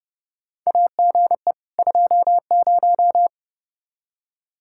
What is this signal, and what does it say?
Morse code, heard as a keyed tone, about 30 wpm, 700 Hz. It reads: AGE 20
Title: Practice QSO - Exchange 3 - More Info - 30wpm